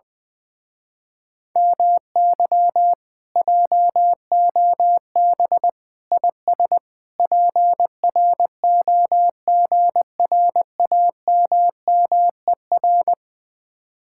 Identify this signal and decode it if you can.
Morse code, heard as a keyed tone, about 20 wpm, 700 Hz. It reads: MY JOB IS PROGRAMMER
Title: Speed-Racing Common QSO Phrases - 20wpm